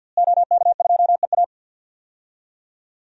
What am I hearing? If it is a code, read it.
Morse code, heard as a keyed tone, about 46 wpm, 700 Hz. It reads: OK2EA
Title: DXCC-100 Call Signs - 5 Characters - Repeated 2x - 50wpm